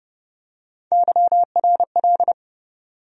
Morse code, heard as a keyed tone, about 30 wpm, 700 Hz. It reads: YRL
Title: USA Call Sign Postfixes - 3 Letters 30wpm